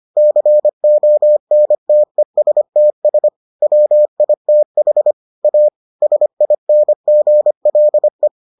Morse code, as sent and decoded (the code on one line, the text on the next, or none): -.-. --- -. - . ... - ... / .-- .. - .... / .- / ... .. -. --. .-.. .
CONTESTS WITH A SINGLE